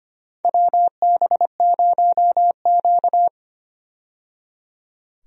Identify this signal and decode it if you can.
Morse code, heard as a keyed tone, about 25 wpm, 700 Hz. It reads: WB0Q
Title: USA Call Signs - 4 Characters - 25wpm